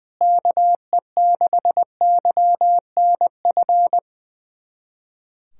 Morse code, heard as a keyed tone, about 19 wpm, 700 Hz. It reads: KE6YNF
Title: USA Callsign DE Callsign 20wpm